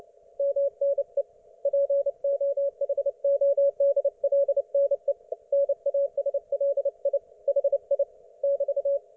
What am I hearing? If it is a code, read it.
Morse code, heard as a keyed tone, about 29 wpm, 560 Hz. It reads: MNE POHODLNE NASLI HI <BT>